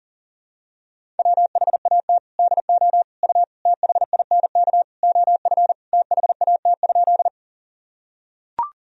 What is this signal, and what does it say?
Morse code, heard as a keyed tone, about 37 wpm, 700 Hz. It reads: WHAT DO U THINK OF THAT?
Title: Sentences from Top 200 Words and Abbreviations 40wpm